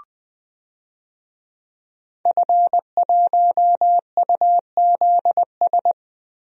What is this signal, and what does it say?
Morse code, heard as a keyed tone, about 20 wpm, 700 Hz. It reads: F1UZS
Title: DXCC-100 Call Signs - 5 Characters - Repeated 3x - 20wpm